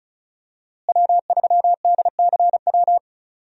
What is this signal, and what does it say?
Morse code, heard as a keyed tone, about 36 wpm, 700 Hz. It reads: W3DCW
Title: USA Call Signs - 5 Characters - Repeated 3x - 35wpm